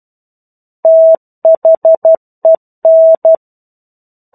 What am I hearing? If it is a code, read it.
Morse code, heard as a keyed tone, about 12 wpm, 650 Hz. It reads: THEN
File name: 01-Chapter-01 12 WPM